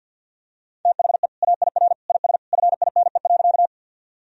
Morse code, heard as a keyed tone, about 50 wpm, 700 Hz. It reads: THE AIR IS FINE.
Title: Sentences from Top 1000 Words - Review Part 1 - 50wpm